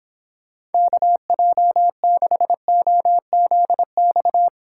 Morse code, heard as a keyed tone, about 26 wpm, 700 Hz. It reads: KJ6OZX